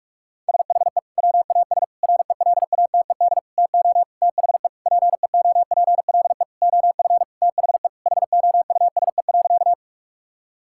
Morse code, heard as a keyed tone, about 46 wpm, 700 Hz. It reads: SHE WAS RELATED TO THE PEOPLE OF THE HOUSE.
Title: Sentences from Top 1000 Words - Review Part 5 - 45wpm